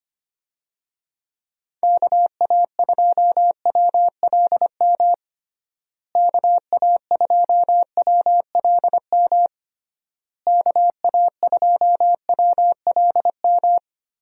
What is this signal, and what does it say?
Morse code, heard as a keyed tone, about 25 wpm, 700 Hz. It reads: KA2WLM KA2WLM KA2WLM
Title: USA Call Signs - 6 Characters - Repeated 3x - 25wpm